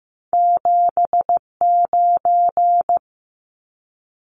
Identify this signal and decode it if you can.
Morse code, heard as a keyed tone, about 15 wpm, 700 Hz. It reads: 79
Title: Numbers - 2-Digit 15wpm